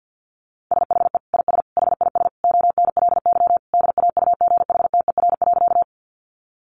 Morse code, heard as a keyed tone, about 50 wpm, 700 Hz. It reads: SHE IS HIS ONLY DAUGHTER.
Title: Sentences from Top 1500 Words 50wpm